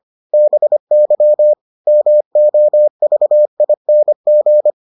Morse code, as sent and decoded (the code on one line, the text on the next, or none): -... -.-- / -- --- ...- .. -. --.
BY MOVING